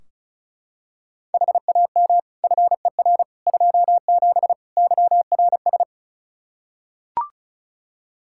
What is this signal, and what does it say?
Morse code, heard as a keyed tone, about 34 wpm, 700 Hz. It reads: HAM FER 27 YRS
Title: Practice QSO - Exchange 3 - More Info - 35wpm